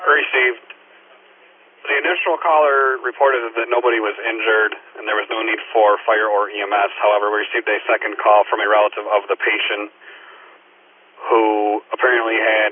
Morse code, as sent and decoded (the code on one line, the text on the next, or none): none